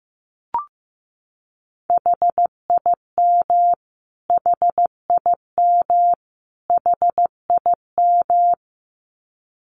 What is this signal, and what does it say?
Morse code, heard as a keyed tone, about 14 wpm, 700 Hz. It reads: HIM HIM HIM
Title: Top 200 Words Repeated 15wpm